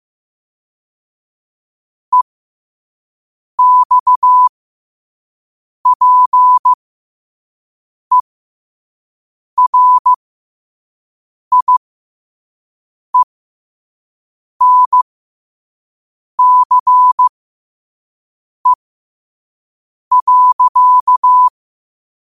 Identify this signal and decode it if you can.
Morse code, heard as a keyed tone, about 15 wpm, spaced as for 5 wpm, 1000 Hz. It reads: EXPERIENCE.
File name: qotdfiles / 5wpm